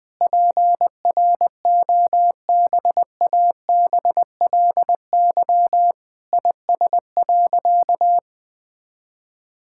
Morse code, heard as a keyed tone, about 20 wpm, 700 Hz. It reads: PROBABLY IS.
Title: Sentences with 9-Word Phrases - 20wpm